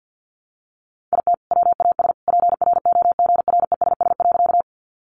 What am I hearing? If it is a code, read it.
Morse code, heard as a keyed tone, about 48 wpm, 700 Hz. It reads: IT WAS PROGRESS.
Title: Sentences from Top 1400 Words 50wpm